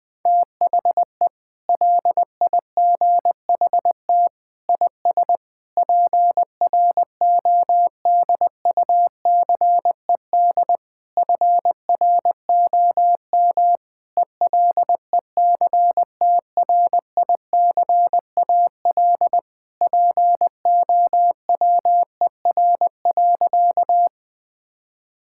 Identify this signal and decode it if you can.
Morse code, heard as a keyed tone, about 20 wpm, 700 Hz. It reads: THE LIGHT IS PRODUCED FROM ELECTRICAL POWER.
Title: Sentences from Top 1000 Words - Review Part 6 - 20wpm